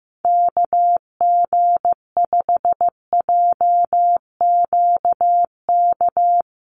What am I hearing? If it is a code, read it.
Morse code, heard as a keyed tone, about 15 wpm, 700 Hz. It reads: KG5JQK
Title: USA Call Signs Repeated 2x 15wpm